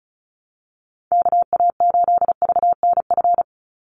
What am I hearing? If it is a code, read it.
Morse code, heard as a keyed tone, about 34 wpm, 700 Hz. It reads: KA8VNF
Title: USA Call Signs 35wpm